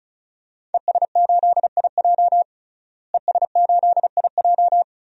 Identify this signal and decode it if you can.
Morse code, heard as a keyed tone, about 36 wpm, 700 Hz. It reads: ES8IJ ES8IJ
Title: DXCC-100 Call Signs - 5 Characters - Repeated 2x - 35wpm